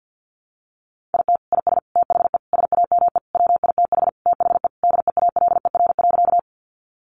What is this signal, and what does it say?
Morse code, heard as a keyed tone, about 50 wpm, 700 Hz. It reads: IT IS THE SAME WITH THE DEALER.
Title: Sentences from Top 1100 Words 50wpm